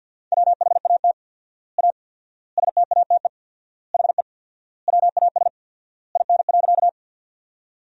Morse code, heard as a keyed tone, about 50 wpm, 700 Hz. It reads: WHAT A STATE HE WAS IN.
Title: 4x Word-Spacing - Sentences from Top 400 Words - 50wpm